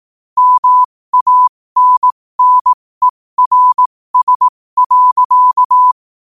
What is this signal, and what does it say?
Morse code, heard as a keyed tone, about 18 wpm, spaced as for 15 wpm, 1000 Hz. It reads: MANNERS.
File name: qotdfiles / 15wpm